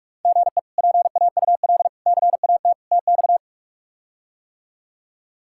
Morse code, heard as a keyed tone, about 45 wpm, 700 Hz. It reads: GE PAUL CAT TX
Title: K1USN SST QSO Practice - 45wpm